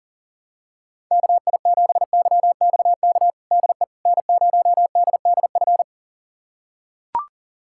Morse code, heard as a keyed tone, about 40 wpm, 700 Hz. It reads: KI7YXK DE N0DDF
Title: USA Callsign DE Callsign 40wpm